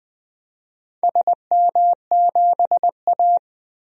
Morse code, heard as a keyed tone, about 21 wpm, 700 Hz. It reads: SM7A
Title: DXCC-100 Call Signs - 4 Characters - 20wpm